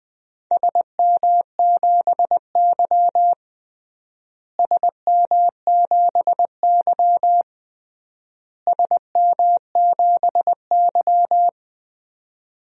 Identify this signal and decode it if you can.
Morse code, heard as a keyed tone, about 20 wpm, 700 Hz. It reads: SM7Y SM7Y SM7Y
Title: DXCC-100 Call Signs - 4 Characters - Repeated 3x - 20wpm